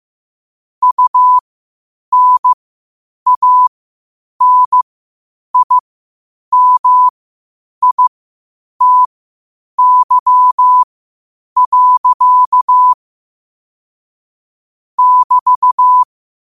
Morse code, heard as a keyed tone, about 15 wpm, spaced as for 8 wpm, 1000 Hz. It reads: UNANIMITY. <BT>
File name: qotdfiles / 8wpm